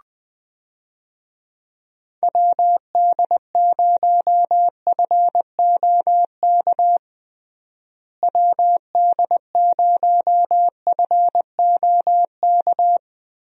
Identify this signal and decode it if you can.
Morse code, heard as a keyed tone, about 20 wpm, 700 Hz. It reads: WD0FOK WD0FOK
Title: USA Call Signs - 6 Characters - Repeated 3x - 20wpm